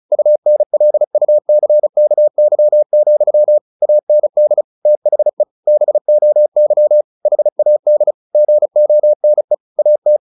Morse code, heard as a keyed tone, about 34 wpm, 600 Hz. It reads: UNLUCKY, AND THE BOY HAD GONE AT
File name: the old man and the sea / the old man and the sea 1 35wpm